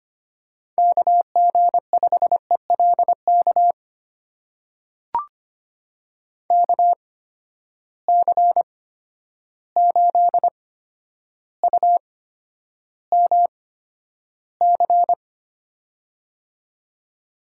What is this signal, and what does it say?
Morse code, heard as a keyed tone, about 25 wpm, spaced as for 11 wpm, 700 Hz. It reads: KG5ELK KC8UMC
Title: USA Call Signs - 6 Characters - 8x Character Spacing - 25wpm